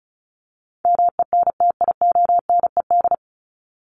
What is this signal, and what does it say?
Morse code, heard as a keyed tone, about 34 wpm, 700 Hz. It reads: MENTIONED
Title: ED Suffix - Top 100 Words with Base Words 35wpm